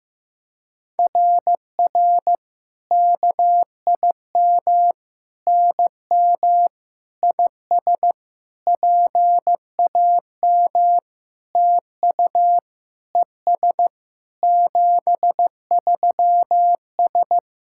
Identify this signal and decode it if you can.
Morse code, heard as a keyed tone, about 15 wpm, 700 Hz. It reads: RR KIM NM IS PAM TU ES 73S
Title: SOTA QSO Practice 15wpm